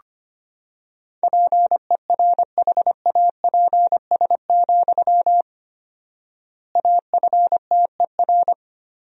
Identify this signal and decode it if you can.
Morse code, heard as a keyed tone, about 25 wpm, 700 Hz. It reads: PERHAPS, AFTER
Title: 4x Word-Spacing - Sentences from Top 500 Words - 25wpm